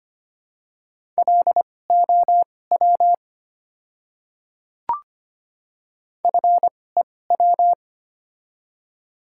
Morse code, heard as a keyed tone, about 25 wpm, spaced as for 18 wpm, 700 Hz. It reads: LOW FEW
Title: 2x Character Spacing - Top 300 Words - 25wpm